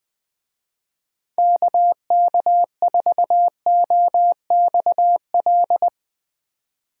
Morse code, heard as a keyed tone, about 20 wpm, 700 Hz. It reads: KK4OXL
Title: USA Call Signs - 6 Characters - Repeated 2x - 20wpm